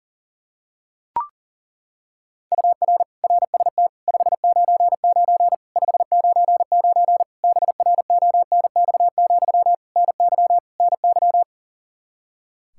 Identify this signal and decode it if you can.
Morse code, heard as a keyed tone, about 40 wpm, 700 Hz. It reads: UR RST 599 599 BRONX, NY NY
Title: Practice QSO - Exchange 1 - Beginning - 40wpm